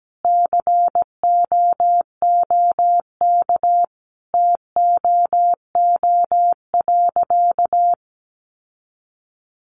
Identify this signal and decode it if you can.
Morse code, heard as a keyed tone, about 17 wpm, 700 Hz. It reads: COOK TOO.
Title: Sentences from Top 700 Words 17wpm